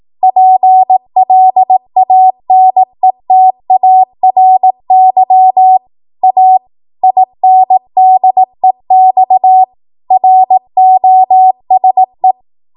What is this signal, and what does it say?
Morse code, heard as a keyed tone, about 18 wpm, 750 Hz. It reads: PLANETARY A INDEX ROSE
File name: Archive / 18 WPM / 18WPM